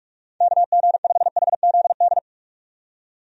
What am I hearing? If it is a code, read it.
Morse code, heard as a keyed tone, about 46 wpm, 700 Hz. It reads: KG5HZD
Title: USA Call Signs - 6 Characters - 45wpm